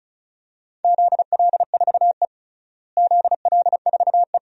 Morse code, heard as a keyed tone, about 36 wpm, 700 Hz. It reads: ZL4E ZL4E
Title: DXCC-100 Call Signs - 4 Characters - Repeated 2x - 35wpm